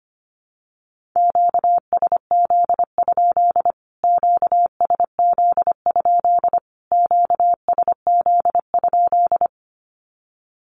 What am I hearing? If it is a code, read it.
Morse code, heard as a keyed tone, about 25 wpm, 700 Hz. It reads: QSZ? QSZ? QSZ?
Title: Q-Codes Repeated 3x 25wpm